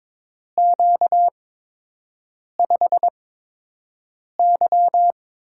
Morse code, heard as a keyed tone, about 21 wpm, spaced as for 6 wpm, 700 Hz. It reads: Q5Y
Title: USA Call Signs - 4 Characters - 8x Character Spacing - 22wpm